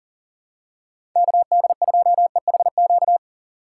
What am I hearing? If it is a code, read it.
Morse code, heard as a keyed tone, about 40 wpm, 700 Hz. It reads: KD2EHQ
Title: USA Call Signs - 6 Characters - 6x Character Spacing - 40wpm